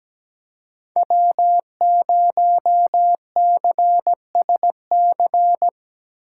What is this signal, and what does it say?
Morse code, heard as a keyed tone, about 18 wpm, 700 Hz. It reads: W0CSC
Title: Speed Racing - USA Call Signs 17wpm